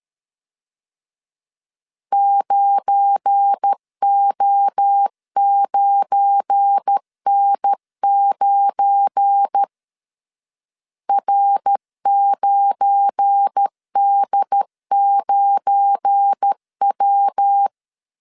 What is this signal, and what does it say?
Morse code, heard as a keyed tone, about 13 wpm, 800 Hz. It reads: 9O9N9 R9D9W
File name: LEZIONI / 7 GF9Z / 7 GF9 12wpm 2319